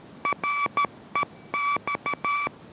Morse code, heard as a keyed tone, about 13 wpm, 1150 Hz, with noise. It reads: REX